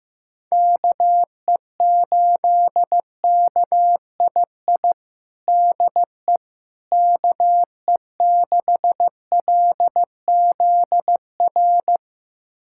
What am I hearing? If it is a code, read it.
Morse code, heard as a keyed tone, about 15 wpm, 700 Hz. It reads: KE8KII DE KE6LZR
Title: USA Callsign DE Callsign - 6 Characters - 15wpm